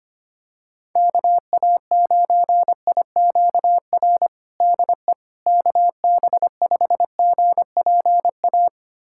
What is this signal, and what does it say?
Morse code, heard as a keyed tone, about 25 wpm, 700 Hz. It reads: KA9IQR DE KB5GPA
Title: USA Callsign DE Callsign - 6 Characters - 25wpm